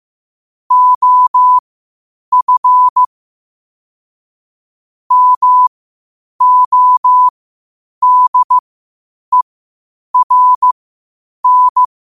Morse code, heard as a keyed tone, about 15 wpm, spaced as for 8 wpm, 1000 Hz. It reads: OF MODERN
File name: qotdfiles / 8wpm